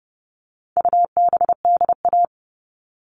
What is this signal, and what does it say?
Morse code, heard as a keyed tone, about 30 wpm, 700 Hz. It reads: UBDA
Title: All Cut Numbers - 4-Digit 30wpm